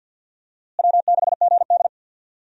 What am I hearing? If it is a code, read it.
Morse code, heard as a keyed tone, about 47 wpm, 700 Hz. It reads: W6GD
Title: USA Call Signs - 4 Characters - Repeated 2x - 50wpm